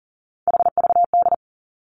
Morse code, heard as a keyed tone, about 41 wpm, 700 Hz. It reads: HVD
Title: Three Challenging Letter-Number Combinations 40wpm